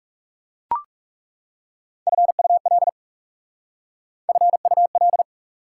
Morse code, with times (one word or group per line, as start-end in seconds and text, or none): 2.1-2.9 FUL
4.3-5.2 FUL